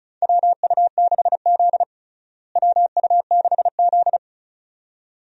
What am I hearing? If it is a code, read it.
Morse code, heard as a keyed tone, about 36 wpm, 700 Hz. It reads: WU6Z WU6Z
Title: USA Call Signs - 4 Characters - Repeated 2x - 35wpm